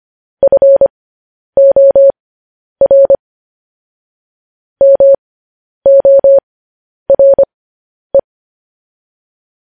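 Morse code, heard as a keyed tone, about 25 wpm, spaced as for 10 wpm, 550 Hz. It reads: FOR MORE
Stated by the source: Morse news headlines at 10.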